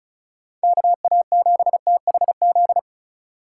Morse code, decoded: KA7THZ

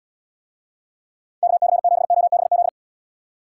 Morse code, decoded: KC7OBZ

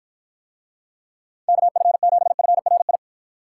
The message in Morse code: KF7FRI

